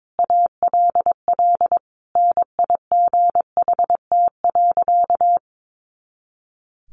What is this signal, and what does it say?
Morse code, heard as a keyed tone, about 22 wpm, 700 Hz. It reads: ALL NIGHT.
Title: Sentences from Top 500 Words 22wpm